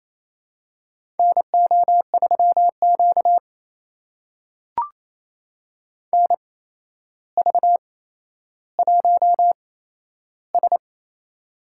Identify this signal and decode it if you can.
Morse code, heard as a keyed tone, about 28 wpm, spaced as for 12 wpm, 700 Hz. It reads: NO3Q NV1S
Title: USA Call Signs - 4 Characters - 8x Character Spacing - 28wpm